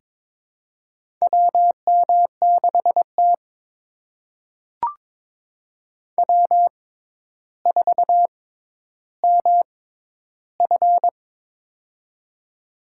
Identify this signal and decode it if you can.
Morse code, heard as a keyed tone, about 22 wpm, spaced as for 11 wpm, 700 Hz. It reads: WM6T W4MF
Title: USA Call Signs - 6x Character Spacing - 22wpm